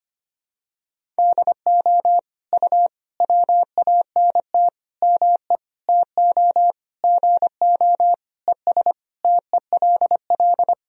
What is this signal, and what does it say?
Morse code, heard as a keyed tone, about 25 wpm, 700 Hz. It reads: DO U WANT ME TO GO ES TELL
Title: Sentences from Top 200 Words and Abbreviations 25wpm